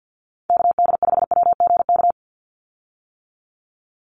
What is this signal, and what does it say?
Morse code, heard as a keyed tone, about 50 wpm, 700 Hz. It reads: KD5WGK
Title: USA Call Signs 50wpm